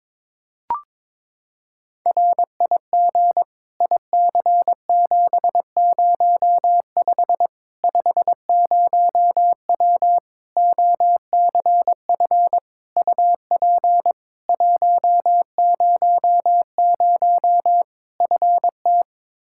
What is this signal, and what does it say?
Morse code, heard as a keyed tone, about 22 wpm, 700 Hz. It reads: RIG IC705 50W OCF UP 100 FT
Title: Practice QSO - Exchange 2 - Info - 22wpm